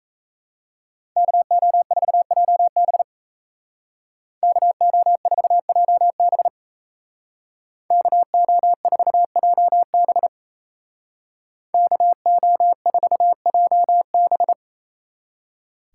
Morse code, heard as a keyed tone, about 36 wpm, 700 Hz. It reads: KO4JB KO4JB KO4JB KO4JB